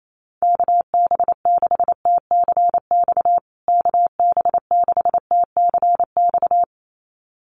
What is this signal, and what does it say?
Morse code, heard as a keyed tone, about 28 wpm, 700 Hz. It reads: KB6TCX KB6TCX
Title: USA Call Signs Repeated 2x 28wpm